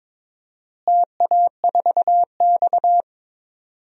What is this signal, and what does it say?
Morse code, heard as a keyed tone, about 22 wpm, 700 Hz. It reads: TA4X
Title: DXCC-100 Call Signs - 4 Characters - Repeated 2x - 22wpm